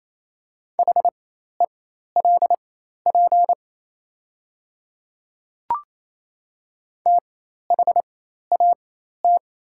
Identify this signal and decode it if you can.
Morse code, heard as a keyed tone, about 28 wpm, spaced as for 13 wpm, 690 Hz. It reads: HELP THAT
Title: 4x Character Spacing - Top 200 Words - 28wpm